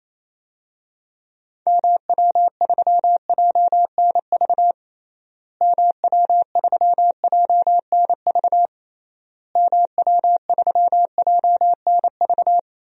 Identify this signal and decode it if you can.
Morse code, heard as a keyed tone, about 28 wpm, 700 Hz. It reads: MW3JNV MW3JNV MW3JNV
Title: DXCC-100 Call Signs - 6 Characters - Repeated 3x - 28wpm